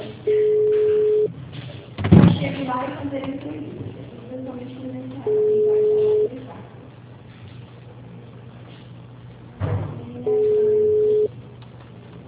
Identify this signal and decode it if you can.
Ringback tone (Europe, 425 Hz), with speech.
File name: out- -2012- - - .5647